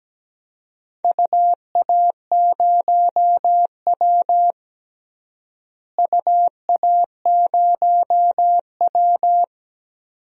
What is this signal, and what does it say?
Morse code, heard as a keyed tone, about 17 wpm, 700 Hz. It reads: UA0W UA0W
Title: DXCC-100 Call Signs - 4 Characters - Repeated 2x - 17wpm